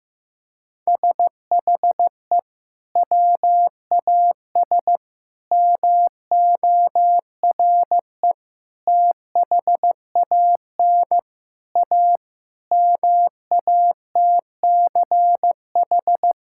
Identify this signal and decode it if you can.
Morse code, heard as a keyed tone, about 15 wpm, 700 Hz. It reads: SHE WAS MORE THAN A MATCH
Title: Sentences from Top 1600 Words 15wpm